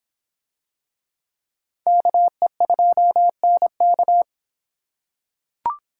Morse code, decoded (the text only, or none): KE2NK